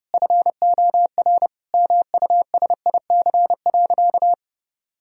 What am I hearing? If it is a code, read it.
Morse code, heard as a keyed tone, about 30 wpm, 700 Hz. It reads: FOR MUSIC.
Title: Sentences from Top 1000 Words - Review Part 1 - 30wpm